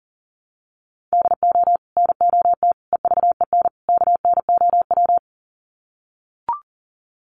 Morse code, heard as a keyed tone, about 40 wpm, 700 Hz. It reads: DO NOT EVEN KNOW
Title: Sets of 4 Words 40wpm